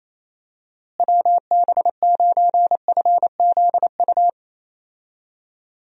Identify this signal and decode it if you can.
Morse code, heard as a keyed tone, about 28 wpm, 700 Hz. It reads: WB9FZU
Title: USA Call Signs - 6 Characters - Repeated 2x - 28wpm